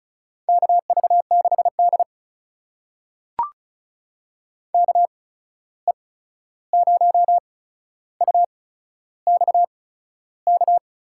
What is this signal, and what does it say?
Morse code, heard as a keyed tone, about 36 wpm, spaced as for 13 wpm, 700 Hz. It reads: KV6D KE0UXK
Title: USA Call Signs - 8x Character Spacing - 35wpm